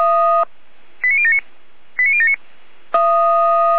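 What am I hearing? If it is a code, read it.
Five-tone selective call (ZVEI), heard as quick runs of five tones, tones of about 70 ms. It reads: 89079 89079
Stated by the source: ZVEI five-tone selective call received on an SDR.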